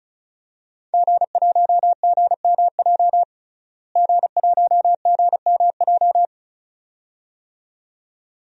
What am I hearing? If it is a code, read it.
Morse code, heard as a keyed tone, about 36 wpm, 700 Hz. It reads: G1GMJ G1GMJ
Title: DXCC-100 Call Signs - 5 Characters - Repeated 2x - 35wpm